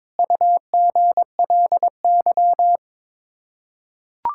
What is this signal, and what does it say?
Morse code, heard as a keyed tone, about 22 wpm, 700 Hz. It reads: UGLY
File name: practice / 22wpm / Sets of 7 Words - 22wpm